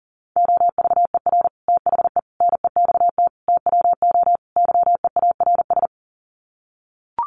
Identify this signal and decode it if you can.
Morse code, heard as a keyed tone, about 40 wpm, 700 Hz. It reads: OVER THE NEXT TWO YEARS
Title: Sets of 5 Words Encore2 40wpm